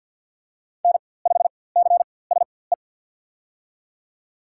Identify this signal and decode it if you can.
Morse code, heard as a keyed tone, about 47 wpm, spaced as for 23 wpm, 700 Hz. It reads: N5CSE
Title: USA Call Signs - 5 Characters - 4x Character Spacing - 50wpm